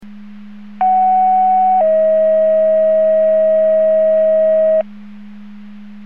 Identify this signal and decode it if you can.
Two-tone page, heard as a one-second tone, then a three-second tone, about 750 then 640 Hz. It is heard from 0.8 to 4.8 s.